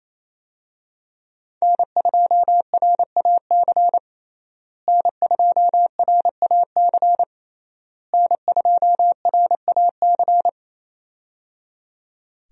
Morse code, heard as a keyed tone, about 28 wpm, 700 Hz. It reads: N2RAC N2RAC N2RAC
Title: USA Call Signs - 5 Characters - Repeated 3x - 28wpm